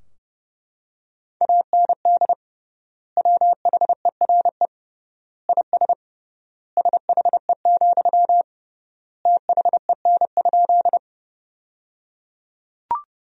Morse code, heard as a keyed tone, about 30 wpm, 700 Hz. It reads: AND WHERE IS SHE, THEN?
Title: 3x Word-Spacing - Sentences from Top 200 Words - 30wpm